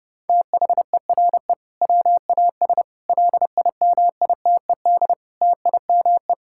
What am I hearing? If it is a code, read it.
Morse code, heard as a keyed tone, about 32 wpm, 700 Hz. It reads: THERE WAS LIMITED TIME
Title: Sentences from Top 1000 Words - Review Part 6 - 30wpm